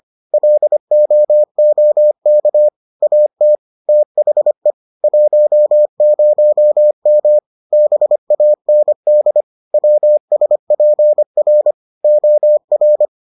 Morse code, heard as a keyed tone, about 25 wpm, 600 Hz. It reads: LOOK AT THE 10M BAND WSPR OR